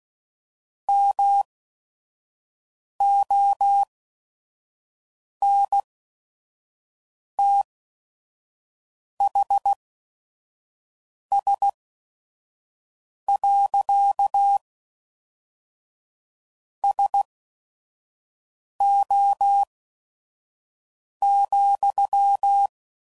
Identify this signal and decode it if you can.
Morse code, heard as a keyed tone, about 16 wpm, spaced as for 5 wpm, 800 Hz. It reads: MONTHS. SO,
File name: CW Code Groups @5WPM / QSOs / WD0FBT 1